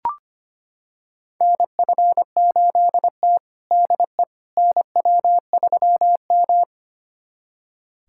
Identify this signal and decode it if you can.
Morse code, heard as a keyed tone, about 25 wpm, 700 Hz. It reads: NF8T DE NW3M
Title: USA Callsign DE Callsign - 4 Characters - 25wpm